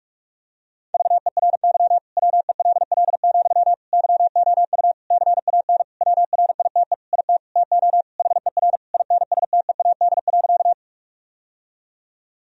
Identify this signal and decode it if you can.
Morse code, heard as a keyed tone, about 46 wpm, 700 Hz. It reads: VERY WELL, YOU CAN WRITE IT TO HER INSTEAD.
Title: Sentences from Top 1000 Words - Review Part 4 - 45wpm